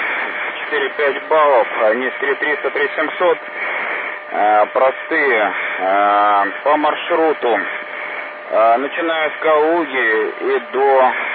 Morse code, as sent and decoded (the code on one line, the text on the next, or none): none